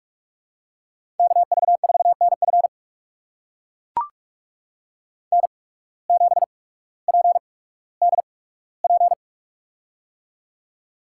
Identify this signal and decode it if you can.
Morse code, heard as a keyed tone, about 44 wpm, spaced as for 19 wpm, 690 Hz. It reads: KV4NF N7PDP
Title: USA Call Signs - 5 Characters - 8x Character Spacing - 45wpm